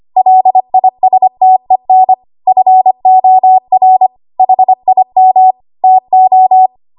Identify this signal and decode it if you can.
Morse code, heard as a keyed tone, about 25 wpm, 750 Hz. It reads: LISTEN FOR HIM TO
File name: Archive / 25 WPM / 25WPM